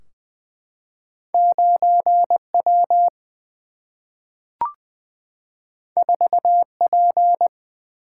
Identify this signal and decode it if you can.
Morse code, heard as a keyed tone, about 20 wpm, 700 Hz. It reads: 9 W 4 P
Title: Single-Letter Single-Number 20wpm